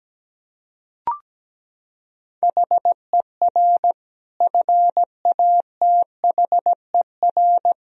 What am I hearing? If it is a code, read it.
Morse code, heard as a keyed tone, about 17 wpm, 700 Hz. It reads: HER FATHER
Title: Sentences from Top 1000 Words - Review Part 1 - 17wpm